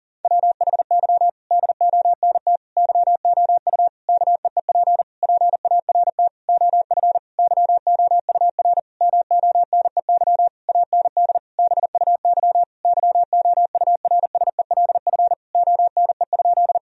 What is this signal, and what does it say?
Morse code, heard as a keyed tone, about 40 wpm, 700 Hz. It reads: WHY DONT YOU KEEP PART OF YOUR MONEY AND BUY YOURSELF ONE?
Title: Sentences from Top 1000 Words - Review Part 4 - 40wpm